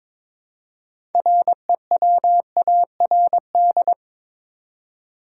Morse code, heard as a keyed tone, about 22 wpm, 700 Hz. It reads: REWARD
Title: Six Letter Words - 22wpm